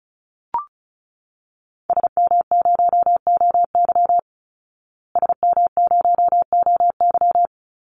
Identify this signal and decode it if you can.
Morse code, heard as a keyed tone, about 34 wpm, 700 Hz. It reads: SM0OY SM0OY
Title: Call Signs Repeated 3x - 35wpm